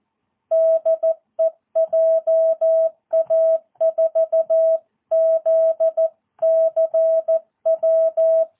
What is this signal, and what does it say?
Morse code, heard as a keyed tone, about 14 wpm, 650 Hz. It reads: DE JA4ZCW
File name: jh8cbh / 2022cw